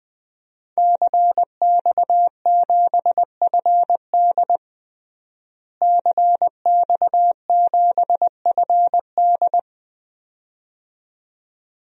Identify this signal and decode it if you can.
Morse code, heard as a keyed tone, about 20 wpm, 700 Hz. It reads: CX7FD CX7FD
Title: DXCC-100 Call Signs - 5 Characters - Repeated 2x - 20wpm